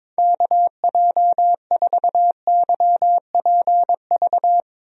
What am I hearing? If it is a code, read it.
Morse code, heard as a keyed tone, about 22 wpm, 700 Hz. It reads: KJ4YPV
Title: USA Call Signs - 6 Characters - Repeated 3x - 22wpm